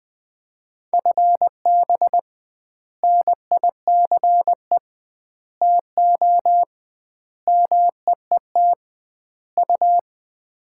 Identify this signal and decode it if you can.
Morse code, heard as a keyed tone, about 20 wpm, 700 Hz. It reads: FB NICE TO MEET U
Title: Common QSO Phrases - 2x Word-Spacing - 20wpm